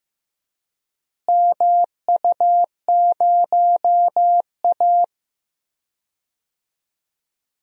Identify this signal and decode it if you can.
Morse code, heard as a keyed tone, about 15 wpm, 700 Hz. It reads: MU0A
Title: DXCC-100 Call Signs - 4 Characters - Repeated 3x - 15wpm